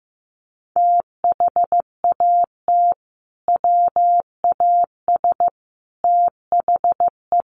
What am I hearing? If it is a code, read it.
Morse code, heard as a keyed tone, about 15 wpm, 700 Hz. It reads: THAT WAS THE
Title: Sentences from Top 1500 Words 15wpm